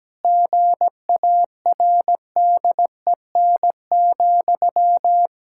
Morse code, heard as a keyed tone, about 17 wpm, 700 Hz. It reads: GARDEN,
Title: Pangram Challenge v2 - 17wpm